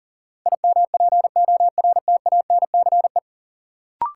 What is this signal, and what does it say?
Morse code, heard as a keyed tone, about 40 wpm, 700 Hz. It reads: IMPORTANCE
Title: Sentences from Top 1200 Words - New Word Drill - 40wpm